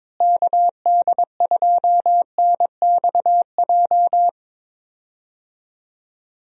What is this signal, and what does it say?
Morse code, heard as a keyed tone, about 22 wpm, 700 Hz. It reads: KD2NXJ
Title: USA Call Signs - 6 Characters - Repeated 2x - 22wpm